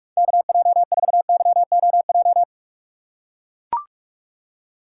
Morse code, heard as a keyed tone, about 42 wpm, 700 Hz. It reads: KJ4YOJ
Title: USA Call Signs - 6 Characters - 3x Character Spacing - 45wpm